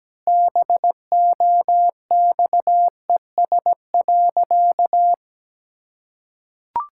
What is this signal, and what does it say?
Morse code, heard as a keyed tone, about 17 wpm, 700 Hz. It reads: BOXES.
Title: Pangram Challenge v2 - 17wpm